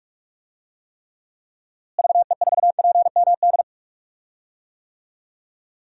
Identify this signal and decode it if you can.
Morse code, heard as a keyed tone, about 42 wpm, 700 Hz. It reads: VE4PMD
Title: DXCC-100 Call Signs - 6 Characters - Repeated 3x - 45wpm